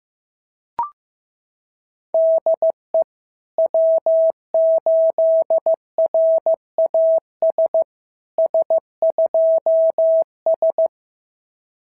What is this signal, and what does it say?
Morse code, heard as a keyed tone, about 14 wpm, 650 Hz. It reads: DE W8RAS S2S
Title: SOTA QSO Practice 15wpm